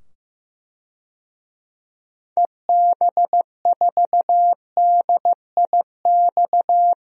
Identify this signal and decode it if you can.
Morse code, heard as a keyed tone, about 15 wpm, 700 Hz. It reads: EB4DIX
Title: DXCC-100 Call Signs - 6 Characters - 15wpm